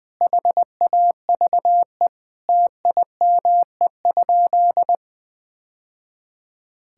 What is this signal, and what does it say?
Morse code, heard as a keyed tone, about 20 wpm, 700 Hz. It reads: HAVE TIME?
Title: Sentences with 3-Word Phrases - 20wpm